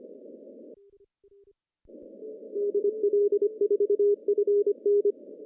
Morse code, heard as a keyed tone, about 24 wpm, 410 Hz. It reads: DL4FN
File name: CW / cqww-cw-k5tr